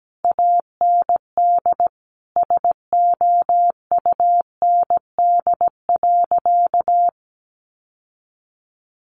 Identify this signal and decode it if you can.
Morse code, heard as a keyed tone, about 17 wpm, 700 Hz. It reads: AND SOUND.
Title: Sentences from Top 900 Words 17wpm